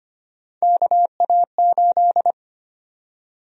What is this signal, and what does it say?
Morse code, heard as a keyed tone, about 25 wpm, 700 Hz. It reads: KA8
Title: USA Call Sign Prefixes - 25wpm